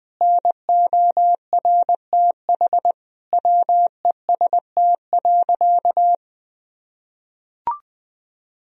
Morse code, heard as a keyed tone, about 21 wpm, 700 Hz. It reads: NORTH WEST.
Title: Sentences from Top 1000 Words - Review Part 2 - 20wpm